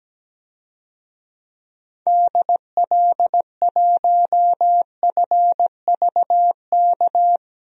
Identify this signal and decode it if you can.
Morse code, heard as a keyed tone, about 17 wpm, 700 Hz. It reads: DL1FVK
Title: DXCC-100 Call Signs - 6 Characters - 17wpm